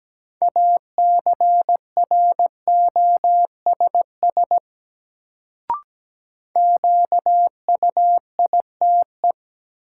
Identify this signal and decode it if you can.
Morse code, heard as a keyed tone, about 17 wpm, 700 Hz. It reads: ACROSS QUITE
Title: Sentences from Top 700 Words - New Word Drill - 17wpm